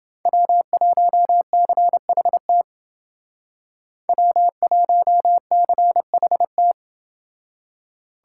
Morse code, heard as a keyed tone, about 29 wpm, 700 Hz. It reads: W1CHT W1CHT